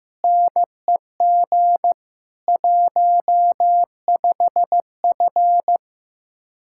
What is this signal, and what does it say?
Morse code, heard as a keyed tone, about 15 wpm, 700 Hz. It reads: NEG 15F
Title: Speed-Racing Common QSO Phrases Encore - 15wpm